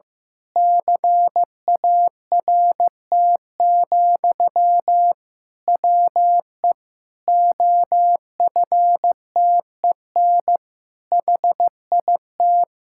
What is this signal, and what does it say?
Morse code, heard as a keyed tone, about 15 wpm, 700 Hz. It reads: CART, WE OFTEN HIT
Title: Pangram Challenge v2 - 15wpm